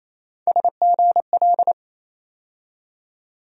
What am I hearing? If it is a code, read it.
Morse code, heard as a keyed tone, about 28 wpm, 700 Hz. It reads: SGL
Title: USA Call Sign Postfixes - 3 Letters 28wpm